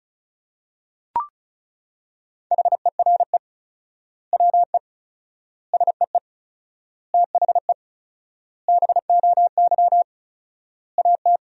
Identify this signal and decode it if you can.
Morse code, heard as a keyed tone, about 34 wpm, 700 Hz. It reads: HERE WE SEE THE BOY AT
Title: 4x Word-Spacing - Sentences from Top 300 Words - 35wpm